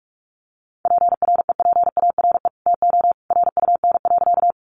Morse code, heard as a keyed tone, about 44 wpm, 700 Hz. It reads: PREPARE TO RUN.
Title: Sentences from Top 900 Words 45wpm